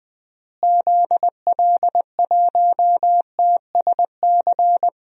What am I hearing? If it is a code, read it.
Morse code, heard as a keyed tone, about 20 wpm, 700 Hz. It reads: ZL1TSC
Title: DXCC-100 Call Signs - 6 Characters - Repeated 3x - 20wpm